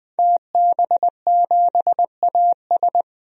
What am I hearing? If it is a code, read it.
Morse code, heard as a keyed tone, about 20 wpm, 700 Hz. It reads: TB7AS
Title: DXCC-100 Call Signs - 5 Characters - 20wpm